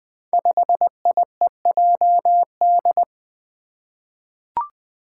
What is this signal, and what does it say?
Morse code, heard as a keyed tone, about 20 wpm, 700 Hz. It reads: 5IEJD
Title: Five Challenging Letter-Number Combinations 20wpm